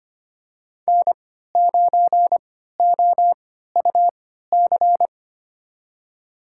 Morse code, heard as a keyed tone, about 25 wpm, spaced as for 14 wpm, 700 Hz. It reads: N9OUC
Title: USA Call Signs - 3x Character Spacing - 25wpm